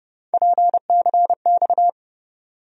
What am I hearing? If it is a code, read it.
Morse code, heard as a keyed tone, about 30 wpm, 700 Hz. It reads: PCX
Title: USA Call Sign Postfixes - 3 Letters 30wpm